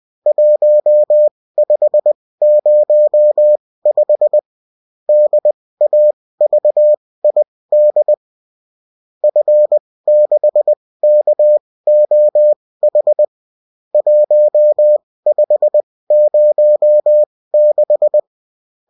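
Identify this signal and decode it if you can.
Morse code, heard as a keyed tone, about 20 wpm, spaced as for 16 wpm, 600 Hz. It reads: 1505 DAVID F6KOH 1506
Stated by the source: Morse code bulletin of a French amateur CW union, sent as text and in CW.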